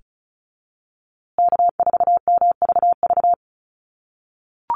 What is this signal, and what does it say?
Morse code, heard as a keyed tone, about 34 wpm, 700 Hz. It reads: K4MVV
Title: USA Call Signs Repeated 3x 35wpm